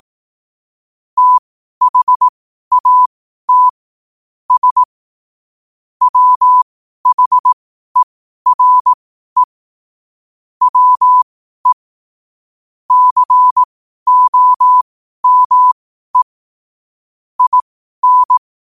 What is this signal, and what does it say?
Morse code, heard as a keyed tone, about 18 wpm, spaced as for 12 wpm, 1000 Hz. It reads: THAT S WHERE WE COME IN
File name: qotdfiles / 12wpm